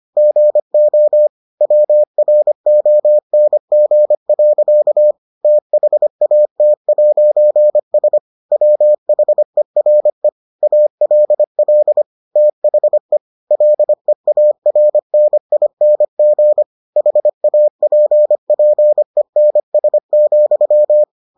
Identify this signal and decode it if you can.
Morse code, heard as a keyed tone, about 25 wpm, 600 Hz. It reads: GO WRONG. THAT'S WHERE ALL THE LEARNING HAPPENS,